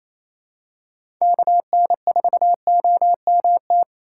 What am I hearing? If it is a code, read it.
Morse code, heard as a keyed tone, about 28 wpm, 700 Hz. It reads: KN4OMT
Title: USA Call Signs - 6 Characters - 4x Character Spacing - 28wpm